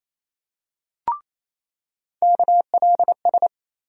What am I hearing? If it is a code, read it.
Morse code, heard as a keyed tone, about 28 wpm, 700 Hz. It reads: KLS
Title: USA Call Sign Postfixes - 3 Letters 28wpm